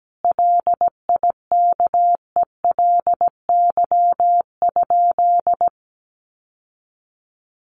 Morse code, heard as a keyed tone, about 17 wpm, 700 Hz. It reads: LIKELY?
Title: Sentences from Top 1100 Words 17wpm